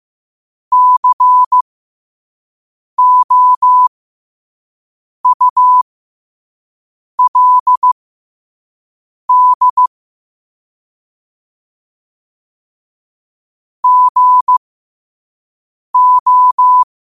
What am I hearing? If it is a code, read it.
Morse code, heard as a keyed tone, about 15 wpm, spaced as for 5 wpm, 1000 Hz. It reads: COULD GO
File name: qotdfiles / 5wpm